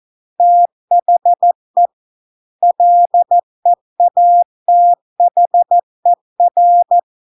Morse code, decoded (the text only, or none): THE LEATHER